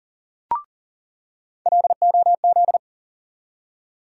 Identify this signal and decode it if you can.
Morse code, heard as a keyed tone, about 37 wpm, 700 Hz. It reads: LOZ
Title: USA Call Sign Postfixes - 3 Letters 40wpm